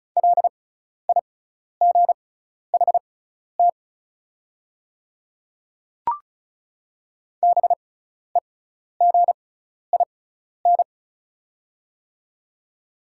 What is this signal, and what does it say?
Morse code, heard as a keyed tone, about 36 wpm, spaced as for 12 wpm, 690 Hz. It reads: LIGHT BEGIN
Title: 6x Character Spacing - Top 300 Words - 35wpm